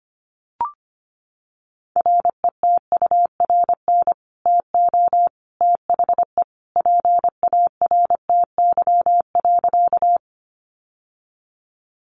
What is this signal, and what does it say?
Morse code, heard as a keyed tone, about 25 wpm, 700 Hz. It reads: RETURN TO THE PARTY.
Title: Sentences from Top 1400 Words 25wpm